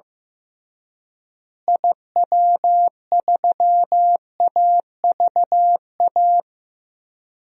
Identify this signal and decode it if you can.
Morse code, heard as a keyed tone, about 16 wpm, 700 Hz. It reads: IW3AVA
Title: DXCC-100 Call Signs - 6 Characters - Repeated 2x - 15wpm